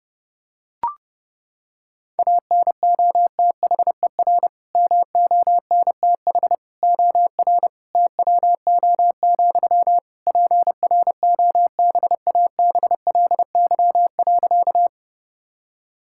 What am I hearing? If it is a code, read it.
Morse code, heard as a keyed tone, about 30 wpm, 700 Hz. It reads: ANOTHER MONTH OR TWO, PROBABLY.
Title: Sentences from Top 1000 Words - Review Part 5 - 30wpm